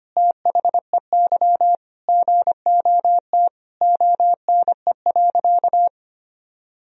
Morse code, heard as a keyed tone, about 25 wpm, 700 Hz. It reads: THEY GOT ONE.
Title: Sentences from Top 300 Words and Abbreviations 25wpm